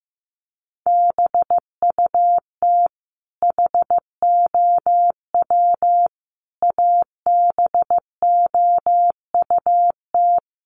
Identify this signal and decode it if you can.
Morse code, heard as a keyed tone, about 15 wpm, 700 Hz. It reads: BUT HOW ABOUT
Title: Sentences from Top 1400 Words 15wpm